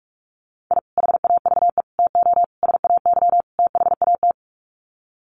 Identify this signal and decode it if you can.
Morse code, heard as a keyed tone, about 44 wpm, 700 Hz. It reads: I HAVE TO SAY THAT
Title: Sets of 5 Words Encore 45wpm